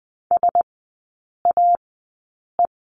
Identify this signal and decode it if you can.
Morse code, heard as a keyed tone, about 20 wpm, spaced as for 8 wpm, 700 Hz. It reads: SAE
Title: 060 Lesson 11e - Letter h - Three Characters 20wpm